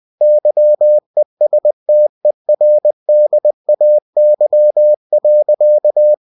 Morse code, heard as a keyed tone, about 20 wpm, 600 Hz. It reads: YESTERDAY.
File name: cw-books / winnie-the-pooh / winnie-20wpm-page-0108